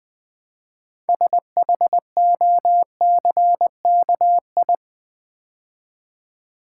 Morse code, heard as a keyed tone, about 20 wpm, 700 Hz. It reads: SHOCKI
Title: Build A Word - 3 Letters At A Time - 8-Letter Words - 20wpm